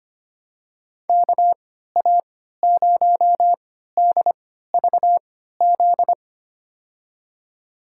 Morse code, heard as a keyed tone, about 25 wpm, spaced as for 14 wpm, 700 Hz. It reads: KA0DVZ
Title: USA Call Signs - 6 Characters - 3x Character Spacing - 25wpm